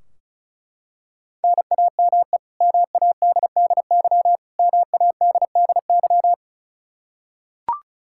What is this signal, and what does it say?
Morse code, heard as a keyed tone, about 36 wpm, 700 Hz. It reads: NAME MADDY MADDY
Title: Practice QSO - Exchange 1 - Beginning - 35wpm